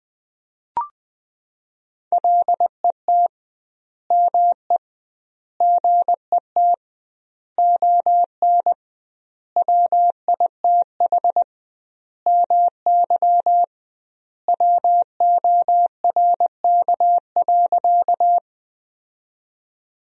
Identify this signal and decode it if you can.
Morse code, heard as a keyed tone, about 21 wpm, 700 Hz. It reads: LET ME GET ON WITH MY WORK.
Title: 2x Word-Spacing - Sentences from Top 300 Words - 20wpm